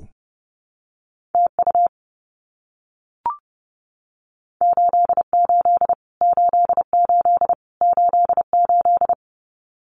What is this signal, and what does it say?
Morse code, heard as a keyed tone, about 28 wpm, 700 Hz. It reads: TU 88 88 88
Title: Abbreviations Repeated 3x - 30wpm